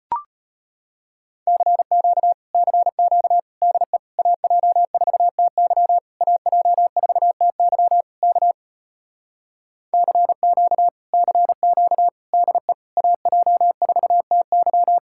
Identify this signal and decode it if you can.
Morse code, heard as a keyed tone, about 33 wpm, 700 Hz. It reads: CQ CQ DE AJ4TY AJ4TY K CQ CQ DE AJ4TY